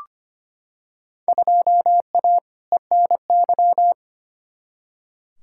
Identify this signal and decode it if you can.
Morse code, heard as a keyed tone, about 25 wpm, 700 Hz. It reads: 2A ENY
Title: Field Day Exchanges - 25wpm